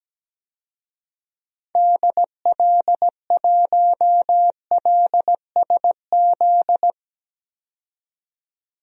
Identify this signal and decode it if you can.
Morse code, heard as a keyed tone, about 17 wpm, 700 Hz. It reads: DL1LSZ
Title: DXCC-100 Call Signs - 6 Characters - 17wpm